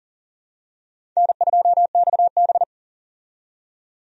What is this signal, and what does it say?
Morse code, heard as a keyed tone, about 40 wpm, 700 Hz. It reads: N2XB
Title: USA Call Signs - 4 Characters - 40wpm